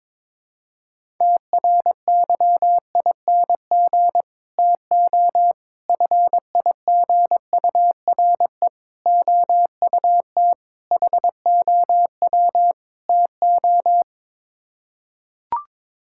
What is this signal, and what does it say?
Morse code, heard as a keyed tone, about 22 wpm, 700 Hz. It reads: TRYING TO FIGURE OUT HOW TO
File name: practice / 22wpm / Sets of 6 Words - 22wpm